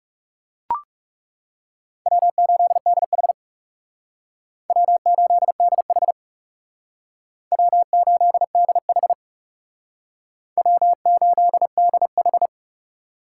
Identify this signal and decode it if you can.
Morse code, heard as a keyed tone, about 36 wpm, 700 Hz. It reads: W8DH W8DH W8DH W8DH
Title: Speed Racing - USA Call Signs 30wpm